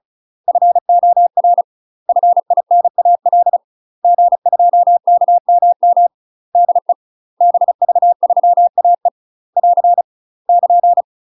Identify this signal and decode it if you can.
Morse code, heard as a keyed tone, about 34 wpm, 700 Hz. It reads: FOR FINAL G2KMM DE BV3AE <AR> <KN>
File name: UK MASTER QSOs TRAINING/35WPM